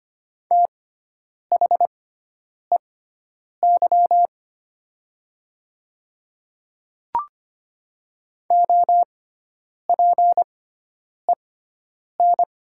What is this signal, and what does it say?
Morse code, heard as a keyed tone, about 25 wpm, spaced as for 9 wpm, 700 Hz. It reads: THEY OPEN
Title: 6x Character Spacing - Top 300 Words - 25wpm